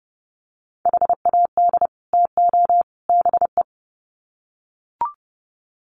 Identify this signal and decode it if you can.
Morse code, heard as a keyed tone, about 30 wpm, 700 Hz. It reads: HAD TO BE
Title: Sets of 3 Words 30wpm